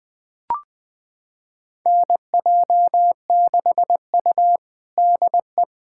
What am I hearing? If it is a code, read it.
Morse code, heard as a keyed tone, about 20 wpm, 700 Hz. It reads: NJ6U DE
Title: USA Callsign DE Callsign - 4 Characters - 20wpm